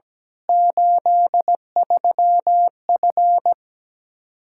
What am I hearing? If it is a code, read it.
Morse code, heard as a keyed tone, about 17 wpm, 700 Hz. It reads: 83F
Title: K1USN SST QSO Practice 17wpm